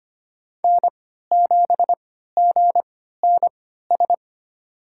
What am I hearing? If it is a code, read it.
Morse code, heard as a keyed tone, about 25 wpm, spaced as for 14 wpm, 700 Hz. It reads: N7GNS
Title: USA Call Signs - 5 Characters - 3x Character Spacing - 25wpm